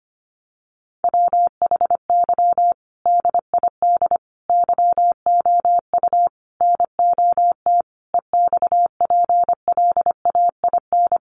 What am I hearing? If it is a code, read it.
Morse code, heard as a keyed tone, about 25 wpm, 700 Hz. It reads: WHY DID YOU NOT EXPLAIN
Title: Sentences from Top 1100 Words 25wpm